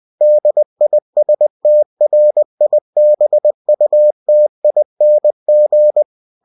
Morse code, heard as a keyed tone, about 20 wpm, 600 Hz. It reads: DISTRIBUTING
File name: cw-books / winnie-the-pooh / winnie-20wpm-page-0116